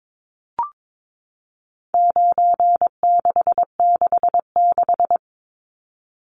Morse code, heard as a keyed tone, about 22 wpm, 700 Hz. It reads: 9666
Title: Numbers - 4-Digit 22wpm